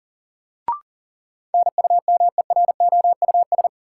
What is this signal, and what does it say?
Morse code, heard as a keyed tone, about 40 wpm, 700 Hz. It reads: NUMEROUS
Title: Sentences from Top 1400 Words - New Word Drill Alt - 40wpm